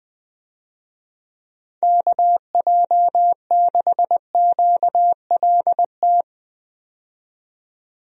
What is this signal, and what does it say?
Morse code, heard as a keyed tone, about 20 wpm, 700 Hz. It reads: KJ6QLT
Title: USA Call Signs - 6 Characters - 20wpm